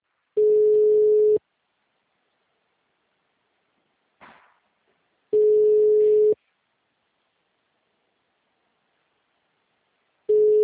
Ringback tone (Europe, 425 Hz).
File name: out- -2003- - - .10